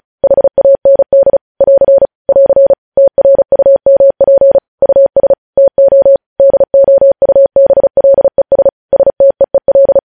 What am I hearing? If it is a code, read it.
Morse code, heard as a keyed tone, about 36 wpm, 550 Hz. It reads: HAND <AR> <AR> TRUMP US TO DOUBLES STEEL
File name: podcast / News headlines-1x-35x35